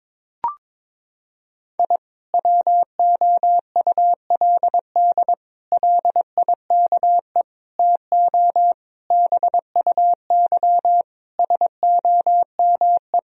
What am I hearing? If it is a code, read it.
Morse code, heard as a keyed tone, about 22 wpm, 700 Hz. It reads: I WOULD LIKE TO BUY SOME
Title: Sentences from Top 1000 Words - Review Part 4 - 22wpm